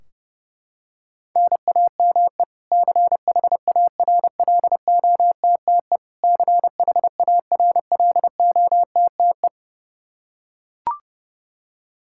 Morse code, heard as a keyed tone, about 30 wpm, 700 Hz. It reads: NAME CHARLOTTE CHARLOTTE
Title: Practice QSO - Exchange 1 - Beginning - 30wpm